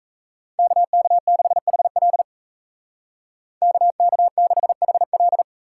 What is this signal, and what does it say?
Morse code, heard as a keyed tone, about 33 wpm, 700 Hz. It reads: KK6HL KK6HL
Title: Speed Racing - USA 2x2 Call Signs 28wpm